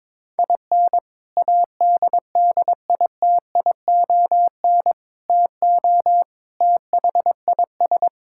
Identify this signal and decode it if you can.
Morse code, heard as a keyed tone, about 22 wpm, 700 Hz. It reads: IN ADDITION TO THIS
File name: practice / 22wpm / Sentences with 3-Word Phrases - 22wpm